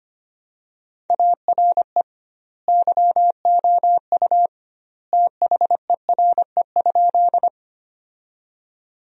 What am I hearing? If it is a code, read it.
Morse code, heard as a keyed tone, about 25 wpm, 700 Hz. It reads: ARE YOU THERE?
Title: 2x Word-Spacing - Sentences from Top 100 Words - 25wpm